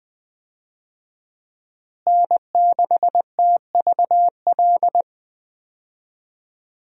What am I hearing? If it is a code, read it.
Morse code, heard as a keyed tone, about 20 wpm, 700 Hz. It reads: N6TVL
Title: USA Call Signs - 5 Characters - Repeated 2x - 20wpm